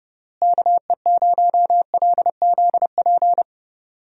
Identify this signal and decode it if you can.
Morse code, heard as a keyed tone, about 30 wpm, 700 Hz. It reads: KE0LZP
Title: USA Call Signs - 6 Characters - Repeated 3x - 30wpm